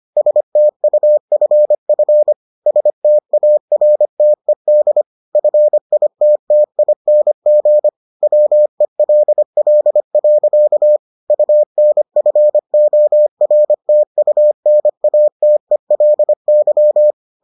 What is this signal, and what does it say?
Morse code, heard as a keyed tone, about 25 wpm, 600 Hz. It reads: STUFF STARTED FITTING WELL. UNFORTUNATELY